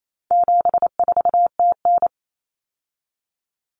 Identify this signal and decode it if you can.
Morse code, heard as a keyed tone, about 28 wpm, 700 Hz. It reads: 74TN
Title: Common Cut Numbers - 4-Digit 28wpm